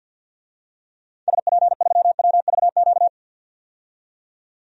Morse code, heard as a keyed tone, about 51 wpm, 700 Hz. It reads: SP3WVQ
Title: DXCC-100 Call Signs - 6 Characters - 50wpm